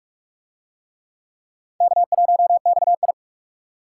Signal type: Morse code

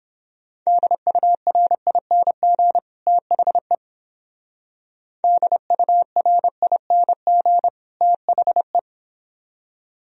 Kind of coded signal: Morse code

